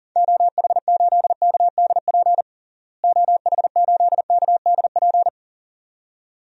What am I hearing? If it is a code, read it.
Morse code, heard as a keyed tone, about 40 wpm, 700 Hz. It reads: OH8KDP OH8KDP
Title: DXCC-100 Call Signs - 6 Characters - Repeated 2x - 40wpm